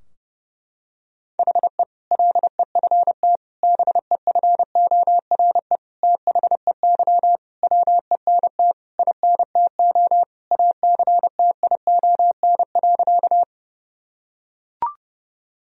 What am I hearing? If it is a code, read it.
Morse code, heard as a keyed tone, about 28 wpm, 700 Hz. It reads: HE LEFT BEFORE THEY WENT INTO ACTION.
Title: Sentences from Top 1000 Words - Review Part 4 - 30wpm